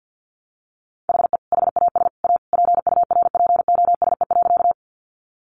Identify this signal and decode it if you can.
Morse code, heard as a keyed tone, about 50 wpm, 700 Hz. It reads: HE HAS A PURPOSE.
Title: Sentences from Top 800 Words 50wpm